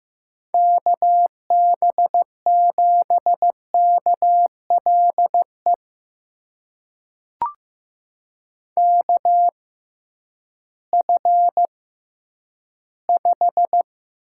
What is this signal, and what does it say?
Morse code, heard as a keyed tone, about 15 wpm, spaced as for 10 wpm, 700 Hz. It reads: KB7KLE KF5
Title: USA Call Signs - 6 Characters - 6x Character Spacing - 15wpm